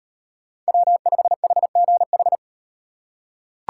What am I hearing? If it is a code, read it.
Morse code, heard as a keyed tone, about 35 wpm, 700 Hz. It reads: W5HGH